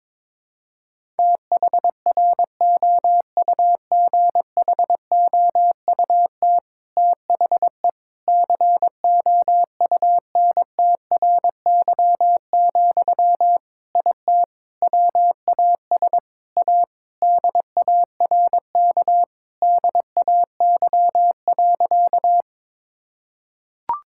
Morse code, heard as a keyed tone, about 22 wpm, 700 Hz. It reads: THROUGHOUT THE COUNTRY, IT WAS A DARK DAY.
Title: Sentences from Top 1000 Words - Review Part 5 - 22wpm